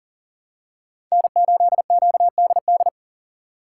Morse code, heard as a keyed tone, about 38 wpm, 700 Hz. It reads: N8QDD
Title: USA Call Signs - 2x Character Spacing - 40wpm